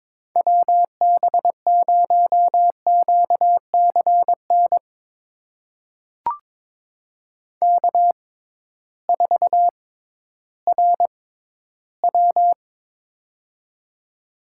Morse code, heard as a keyed tone, about 22 wpm, spaced as for 13 wpm, 700 Hz. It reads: WB0QCN K4RW
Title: USA Call Signs - 6x Character Spacing - 22wpm